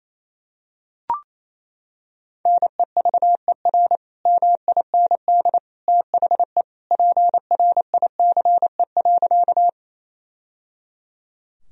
Morse code, heard as a keyed tone, about 28 wpm, 700 Hz. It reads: NEVER MIND THE PRICE.
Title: Sentences from Top 1000 Words - Review Part 4 - 28wpm